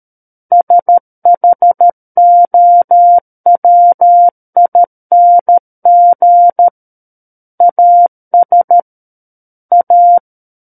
Morse code, heard as a keyed tone, about 13 wpm, 700 Hz. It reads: SHOWING AS A
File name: Book One / 04-Chapter 4 - 13 WPM